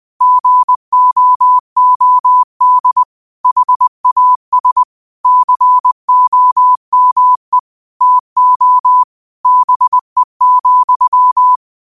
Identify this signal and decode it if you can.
Morse code, heard as a keyed tone, about 20 wpm, 1000 Hz. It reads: GOOD HAS COME TO BE,